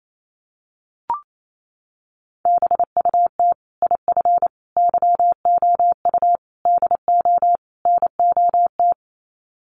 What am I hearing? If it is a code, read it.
Morse code, heard as a keyed tone, about 28 wpm, 700 Hz. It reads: BUT IF YOU DO NOT
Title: Sets of 5 Words Encore 28wpm